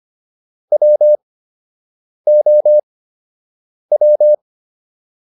Morse code, heard as a keyed tone, about 25 wpm, spaced as for 7 wpm, 600 Hz. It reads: WOW